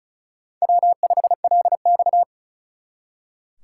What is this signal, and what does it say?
Morse code, heard as a keyed tone, about 46 wpm, 700 Hz. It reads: W5LX